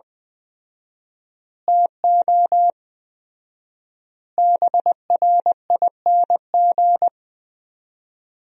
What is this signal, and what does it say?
Morse code, heard as a keyed tone, about 20 wpm, 700 Hz. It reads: TO BRING
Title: 4x Word-Spacing - Sentences from Top 500 Words - 20wpm